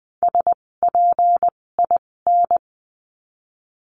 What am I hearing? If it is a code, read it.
Morse code, heard as a keyed tone, about 20 wpm, spaced as for 16 wpm, 700 Hz. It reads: SPIN
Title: 256 Lesson 42a - Words 12-20wpm